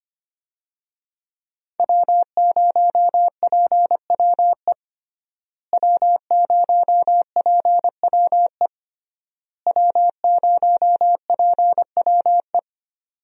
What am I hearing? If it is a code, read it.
Morse code, heard as a keyed tone, about 25 wpm, 700 Hz. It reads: W0PWE W0PWE W0PWE
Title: USA Call Signs - 5 Characters - Repeated 3x - 25wpm